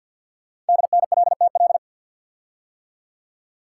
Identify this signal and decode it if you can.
Morse code, heard as a keyed tone, about 50 wpm, 700 Hz. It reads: DNLTL
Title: Code Groups - 5 Letters - 50wpm